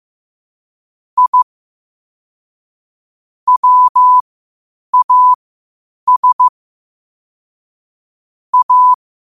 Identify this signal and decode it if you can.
Morse code, heard as a keyed tone, about 15 wpm, spaced as for 8 wpm, 1000 Hz. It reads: I WAS A
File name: qotdfiles / 8wpm